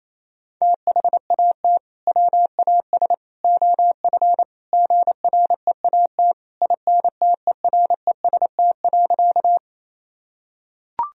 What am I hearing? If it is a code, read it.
Morse code, heard as a keyed tone, about 28 wpm, 700 Hz. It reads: THAT WAS OF GREAT INTEREST.
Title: Sentences from Top 1000 Words - Review Part 2 - 28wpm